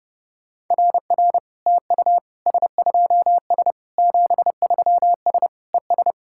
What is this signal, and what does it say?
Morse code, heard as a keyed tone, about 32 wpm, 700 Hz. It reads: RR TU S2S 73S ES
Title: SOTA QSO Practice 30wpm